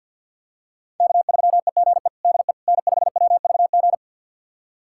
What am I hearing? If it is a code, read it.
Morse code, heard as a keyed tone, about 50 wpm, 700 Hz. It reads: K3EGE DE N5WVG
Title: USA Callsign DE Callsign - 5 Characters - 50wpm